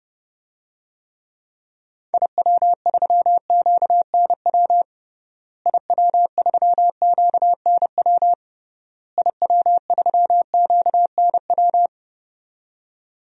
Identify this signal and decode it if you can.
Morse code, heard as a keyed tone, about 32 wpm, 700 Hz. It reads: IW3QNW IW3QNW IW3QNW
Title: DXCC-100 Call Signs - 6 Characters - Repeated 3x - 30wpm